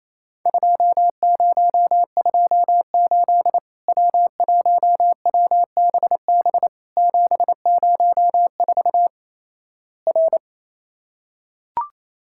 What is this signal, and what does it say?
Morse code, heard as a keyed tone, about 28 wpm, 700 Hz. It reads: 2028 W1WBB 704 R
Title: Worked All Europe DX Contest - QTC Exchange - 28wpm